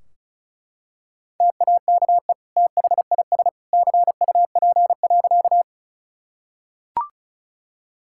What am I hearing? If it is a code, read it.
Morse code, heard as a keyed tone, about 36 wpm, 700 Hz. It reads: TAKE THIS CUP.
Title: Sentences from Top 1600 Words 35wpm